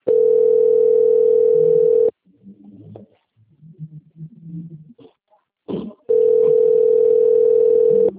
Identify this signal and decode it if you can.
Ringback tone (North America), 440 + 480 Hz.
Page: Toledo - PR